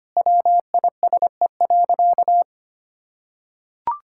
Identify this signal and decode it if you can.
Morse code, heard as a keyed tone, about 25 wpm, 700 Hz. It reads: WISE.
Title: Sentences from Top 1600 Words 25wpm